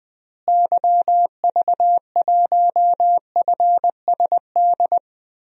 Morse code, heard as a keyed tone, about 20 wpm, 700 Hz. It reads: YV1FSD
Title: DXCC-100 Call Signs - 6 Characters - Repeated 2x - 20wpm